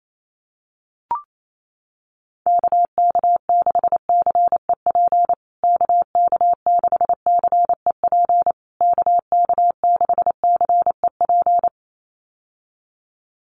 Morse code, heard as a keyed tone, about 28 wpm, 700 Hz. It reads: KK6CEP KK6CEP KK6CEP
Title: USA Call Signs Repeated 3x 28wpm